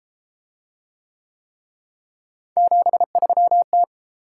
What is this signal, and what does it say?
Morse code, heard as a keyed tone, about 33 wpm, 700 Hz. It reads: 73T